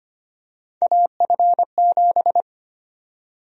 Morse code, heard as a keyed tone, about 25 wpm, 700 Hz. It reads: AF7
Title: USA Call Sign Prefixes - 25wpm